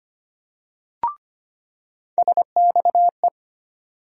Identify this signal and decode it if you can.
Morse code, heard as a keyed tone, about 25 wpm, 700 Hz. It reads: SXE
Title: USA Call Sign Postfixes - 3 Letters 25wpm